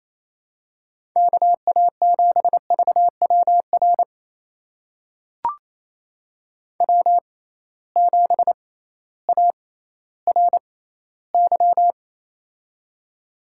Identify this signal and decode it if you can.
Morse code, heard as a keyed tone, about 28 wpm, spaced as for 15 wpm, 700 Hz. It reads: KA7VWR W7ARY
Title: USA Call Signs - 6x Character Spacing - 28wpm